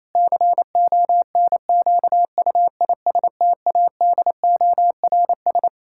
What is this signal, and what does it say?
Morse code, heard as a keyed tone, about 28 wpm, 700 Hz. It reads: CONQUISTADORS
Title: Pangram Challenge v2 - 28wpm